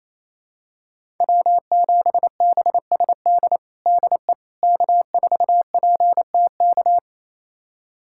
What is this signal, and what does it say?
Morse code, heard as a keyed tone, about 27 wpm, 700 Hz. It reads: W7BSD DE K4PTK
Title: USA Callsign DE Callsign - 5 Characters - 28wpm